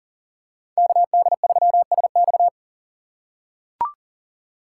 Morse code, heard as a keyed tone, about 40 wpm, 700 Hz. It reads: KD3SX
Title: USA Call Signs - 5 Characters - 2x Character Spacing - 40wpm